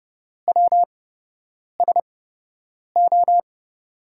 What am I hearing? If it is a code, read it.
Morse code, heard as a keyed tone, about 29 wpm, spaced as for 8 wpm, 700 Hz. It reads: WSO
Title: USA Call Signs - 5 Characters - 8x Character Spacing - 30wpm